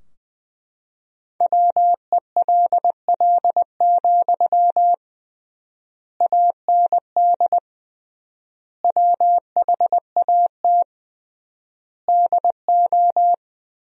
Morse code, heard as a keyed tone, about 20 wpm, 700 Hz. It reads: WELL, AND WHAT DO
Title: 3x Word-Spacing - Sentences from Top 200 Words - 20wpm